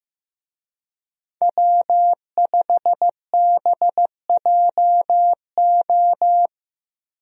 Morse code, heard as a keyed tone, about 15 wpm, 700 Hz. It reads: W5BJO
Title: USA Call Signs - 5 Characters - Repeated 3x - 15wpm